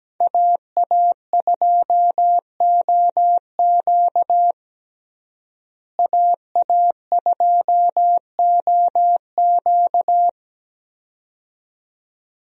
Morse code, heard as a keyed tone, about 17 wpm, 700 Hz. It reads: AA2OQ AA2OQ
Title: USA Call Signs - 5 Characters - Repeated 2x - 17wpm